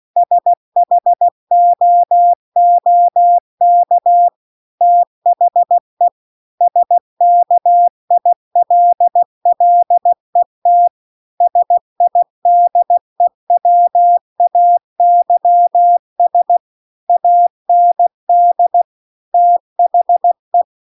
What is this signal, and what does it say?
Morse code, heard as a keyed tone, about 16 wpm, 700 Hz. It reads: SHOOK THE SKILLET SIDEWAYS AND THE